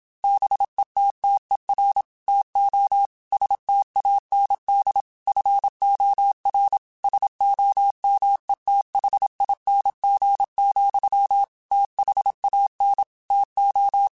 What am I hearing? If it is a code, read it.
Morse code, heard as a keyed tone, about 26 wpm, 800 Hz. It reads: BETTER TO STAND FOR SOMETHING, THAN TO